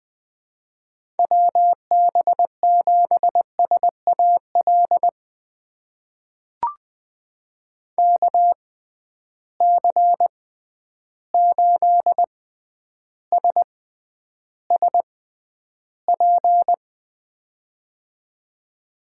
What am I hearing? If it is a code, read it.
Morse code, heard as a keyed tone, about 20 wpm, spaced as for 10 wpm, 700 Hz. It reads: WB7SAL KC8SSP
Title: USA Call Signs - 6 Characters - 6x Character Spacing - 20wpm